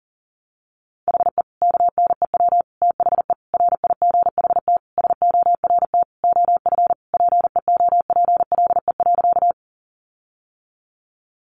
Morse code, heard as a keyed tone, about 37 wpm, 700 Hz. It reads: HE KNEW THE RIGHT SORT OF PEOPLE.
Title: Sentences from Top 1300 Words 40wpm